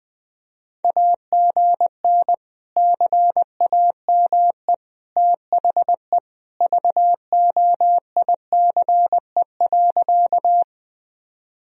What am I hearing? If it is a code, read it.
Morse code, heard as a keyed tone, about 19 wpm, 700 Hz. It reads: AGN CAME THE VOICE.
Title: Sentences from Top 500 Words and Abbreviations 20wpm